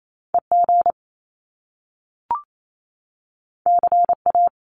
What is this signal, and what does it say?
Morse code, heard as a keyed tone, about 28 wpm, 700 Hz. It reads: EG CA
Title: Top Two Letter Combinations in Words 28wpm